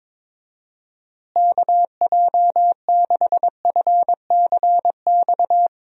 Morse code, heard as a keyed tone, about 22 wpm, 700 Hz. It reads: KJ6FCX
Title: USA Call Signs - 6 Characters - Repeated 3x - 22wpm